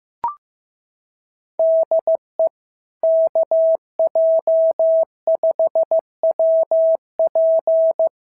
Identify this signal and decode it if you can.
Morse code, heard as a keyed tone, about 15 wpm, 650 Hz. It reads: DE KJ5WP
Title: SOTA QSO Practice 15wpm